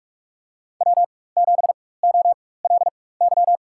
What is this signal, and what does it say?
Morse code, heard as a keyed tone, about 44 wpm, spaced as for 21 wpm, 690 Hz. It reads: W7OLY
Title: USA Call Signs - 4x Character Spacing - 45wpm